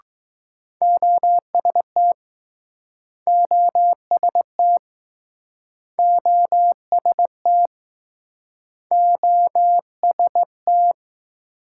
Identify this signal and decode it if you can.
Morse code, heard as a keyed tone, about 21 wpm, 700 Hz. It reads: OST OST OST OST